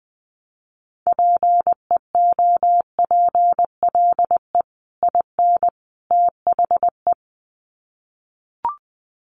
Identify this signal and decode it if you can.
Morse code, heard as a keyed tone, about 20 wpm, 700 Hz. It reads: PEOPLE IN THE
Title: Sets of 3 Words 20wpm